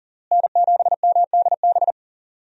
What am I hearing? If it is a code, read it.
Morse code, heard as a keyed tone, about 37 wpm, 700 Hz. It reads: N7MDB
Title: USA Call Signs - 5 Characters - Repeated 3x - 40wpm